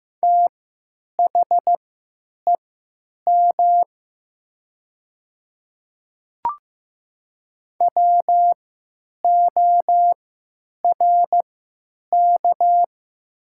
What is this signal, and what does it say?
Morse code, heard as a keyed tone, about 15 wpm, spaced as for 8 wpm, 700 Hz. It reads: THEM WORK
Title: 3x Character Spacing - Top 200 Words - 15wpm